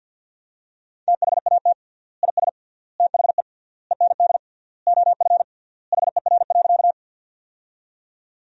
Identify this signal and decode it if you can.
Morse code, heard as a keyed tone, about 50 wpm, 700 Hz. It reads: THAT IS THE END OF HER.
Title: 3x Word-Spacing - Sentences from Top 200 Words - 50wpm